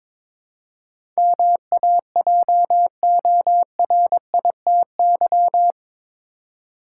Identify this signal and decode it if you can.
Morse code, heard as a keyed tone, about 22 wpm, 700 Hz. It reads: MAJORITY
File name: practice / 22wpm / Eight Letter Words - 22wpm